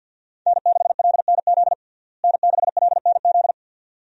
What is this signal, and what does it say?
Morse code, heard as a keyed tone, about 52 wpm, 700 Hz. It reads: N6LNZ N6LNZ
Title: USA Call Signs - 5 Characters - Repeated 2x - 50wpm